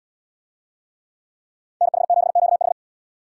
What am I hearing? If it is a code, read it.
Morse code, heard as a keyed tone, about 70 wpm, 700 Hz. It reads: NH7QH